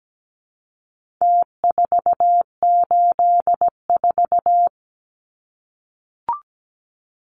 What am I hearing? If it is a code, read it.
Morse code, heard as a keyed tone, about 17 wpm, 700 Hz. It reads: T484
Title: Numbers - Contest Style - 4-Digit 17wpm